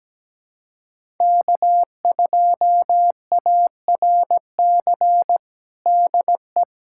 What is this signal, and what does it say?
Morse code, heard as a keyed tone, about 17 wpm, 700 Hz. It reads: K2ARC DE
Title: USA Callsign DE Callsign - 5 Characters - 17wpm